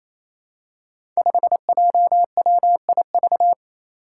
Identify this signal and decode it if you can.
Morse code, heard as a keyed tone, about 29 wpm, 700 Hz. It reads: 5JWIV
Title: Five Challenging Letter-Number Combinations 28wpm